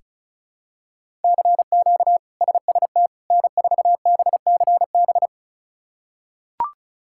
Morse code, heard as a keyed tone, about 36 wpm, 700 Hz. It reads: CQ SST N4BCB
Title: K1USN SST QSO Practice - 35wpm